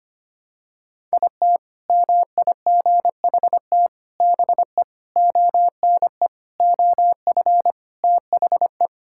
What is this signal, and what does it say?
Morse code, heard as a keyed tone, about 25 wpm, 700 Hz. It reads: IT MIGHT BE ONE OF THE
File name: practice / 25wpm / Sentences with 4-Word Phrases - 25wpm